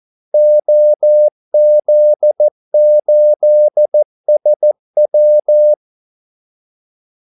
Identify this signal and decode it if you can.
Morse code, heard as a keyed tone, about 14 wpm, 600 Hz. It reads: OZ8SW
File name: Callsigns-14wpm